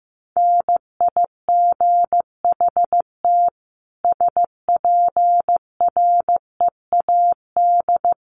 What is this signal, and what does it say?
Morse code, heard as a keyed tone, about 15 wpm, 700 Hz. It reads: NIGHT SPREAD
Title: Sentences from Top 1200 Words 15wpm